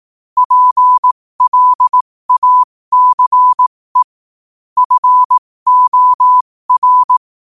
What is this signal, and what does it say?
Morse code, heard as a keyed tone, about 18 wpm, spaced as for 15 wpm, 1000 Hz. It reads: PLACE FOR
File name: qotdfiles / 15wpm